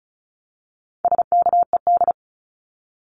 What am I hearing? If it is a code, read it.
Morse code, heard as a keyed tone, about 34 wpm, 700 Hz. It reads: SKED
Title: Abbreviations Repeated 3x - 35wpm